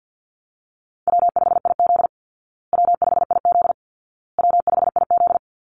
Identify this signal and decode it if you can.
Morse code, heard as a keyed tone, about 48 wpm, 700 Hz. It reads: W5IZ W5IZ W5IZ
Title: Easy Call Signs Repeated 3x - 50wpm